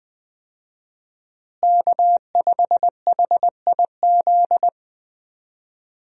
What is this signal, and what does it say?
Morse code, heard as a keyed tone, about 21 wpm, 700 Hz. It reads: K5HIZ
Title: USA Call Signs - 5 Characters - Repeated 2x - 20wpm